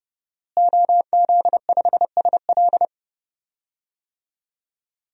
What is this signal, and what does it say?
Morse code, heard as a keyed tone, about 30 wpm, 700 Hz. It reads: OZ5SL
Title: DXCC-100 Call Signs - 5 Characters - Repeated 2x - 30wpm